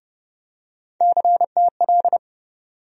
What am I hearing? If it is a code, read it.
Morse code, heard as a keyed tone, about 32 wpm, 700 Hz. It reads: CTL
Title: USA Call Sign Postfixes - 3 Letters 30wpm